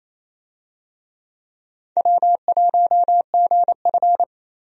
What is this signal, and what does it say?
Morse code, heard as a keyed tone, about 28 wpm, 700 Hz. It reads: W1GF
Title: USA Call Signs - 4 Characters - 28wpm